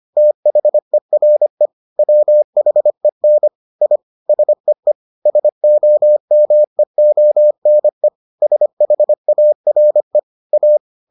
Morse code, heard as a keyed tone, about 25 wpm, 600 Hz. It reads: THERE WHEN I SEE SOMEONE SHARE A